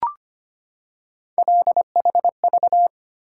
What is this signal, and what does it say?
Morse code, heard as a keyed tone, about 25 wpm, 700 Hz. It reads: LHV
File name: Three Letter-Number Combinations - 25wpm